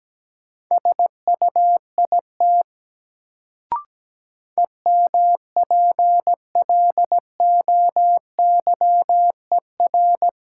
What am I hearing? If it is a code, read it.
Morse code, heard as a keyed tone, about 17 wpm, 700 Hz. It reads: SUIT EMPLOYER
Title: Sentences from Top 1600 Words - New Word Drill Alt - 17wpm